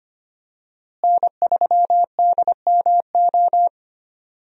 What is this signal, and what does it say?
Morse code, heard as a keyed tone, about 25 wpm, 700 Hz. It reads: N3DMO
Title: USA Call Signs - 5 Characters - Repeated 3x - 25wpm